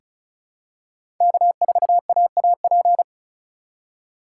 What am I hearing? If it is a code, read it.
Morse code, heard as a keyed tone, about 36 wpm, 700 Hz. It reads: K4AAP
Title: USA Call Signs - 4x Character Spacing - 35wpm